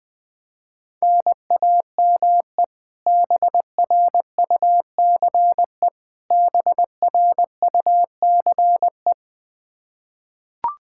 Morse code, heard as a keyed tone, about 21 wpm, 700 Hz. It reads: NAME BRUCE BRUCE
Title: Practice QSO - Exchange 1 - Beginning - 20wpm